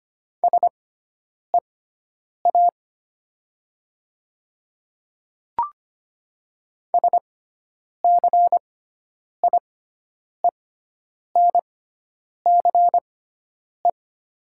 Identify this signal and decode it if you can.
Morse code, heard as a keyed tone, about 25 wpm, spaced as for 9 wpm, 690 Hz. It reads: SEA SCIENCE
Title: 6x Character Spacing - Top 500 Words - 25wpm